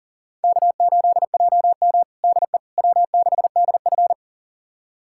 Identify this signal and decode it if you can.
Morse code, heard as a keyed tone, about 40 wpm, 700 Hz. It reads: K8JM DE W6DF
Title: USA Callsign DE Callsign - 4 Characters - 40wpm